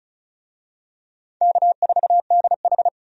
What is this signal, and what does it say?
Morse code, heard as a keyed tone, about 36 wpm, 700 Hz. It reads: K4DH
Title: USA Call Signs - 4 Characters - 35wpm